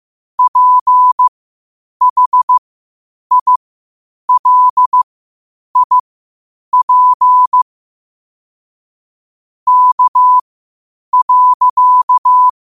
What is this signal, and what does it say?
Morse code, heard as a keyed tone, about 15 wpm, spaced as for 8 wpm, 1000 Hz. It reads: PHILIP K.
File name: qotdfiles / 8wpm